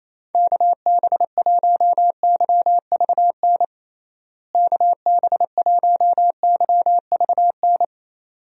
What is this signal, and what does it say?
Morse code, heard as a keyed tone, about 27 wpm, 700 Hz. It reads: KB1YVN KB1YVN
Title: USA Call Signs - 6 Characters - Repeated 2x - 28wpm